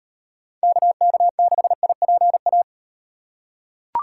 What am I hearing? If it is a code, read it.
Morse code, heard as a keyed tone, about 38 wpm, 700 Hz. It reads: KK6IPA